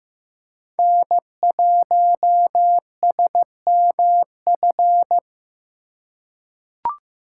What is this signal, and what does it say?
Morse code, heard as a keyed tone, about 15 wpm, 700 Hz. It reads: N1SMF
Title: USA Call Signs - 5 Characters - 6x Character Spacing - 15wpm